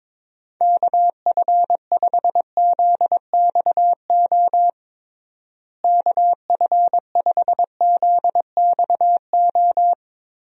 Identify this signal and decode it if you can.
Morse code, heard as a keyed tone, about 22 wpm, 700 Hz. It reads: KF5ZXO KF5ZXO
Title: USA Call Signs - 6 Characters - Repeated 3x - 22wpm